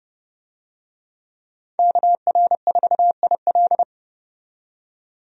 Morse code, heard as a keyed tone, about 30 wpm, 700 Hz. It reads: KR4IL
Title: USA Call Signs - 5 Characters - 30wpm